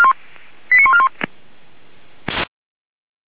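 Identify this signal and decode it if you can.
Five-tone selective call (ZVEI), heard as quick runs of five tones, tones of about 70 ms. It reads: 89141
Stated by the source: ZVEI five-tone selective call received with an SDR, hosted by Feuerwehr Orlamünde.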